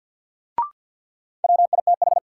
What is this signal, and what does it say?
Morse code, heard as a keyed tone, about 53 wpm, 700 Hz. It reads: WITH
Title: Sets of 2 Words - New Word Drill - 50wpm